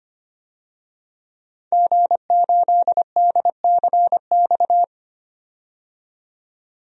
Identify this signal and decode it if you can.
Morse code, heard as a keyed tone, about 25 wpm, 700 Hz. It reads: G8DCX
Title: DXCC-100 Call Signs - 5 Characters - 25wpm